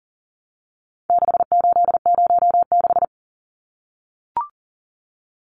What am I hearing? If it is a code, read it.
Morse code, heard as a keyed tone, about 38 wpm, 700 Hz. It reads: 6806
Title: Numbers - 4-Digit 40wpm